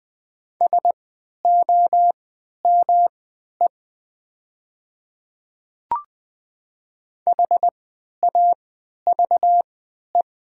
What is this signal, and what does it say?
Morse code, heard as a keyed tone, about 20 wpm, spaced as for 11 wpm, 700 Hz. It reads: SOME HAVE
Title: 3x Character Spacing - Top 200 Words - 20wpm